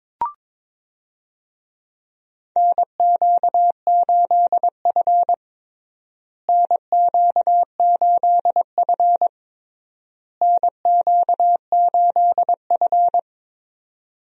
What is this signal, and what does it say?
Morse code, heard as a keyed tone, about 22 wpm, 700 Hz. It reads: NQ8F NQ8F NQ8F
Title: USA Call Signs - 4 Characters - Repeated 3x - 22wpm